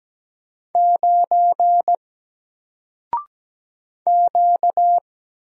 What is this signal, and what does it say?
Morse code, heard as a keyed tone, about 17 wpm, 700 Hz. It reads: 9 Q
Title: Single Letter-Number 17wpm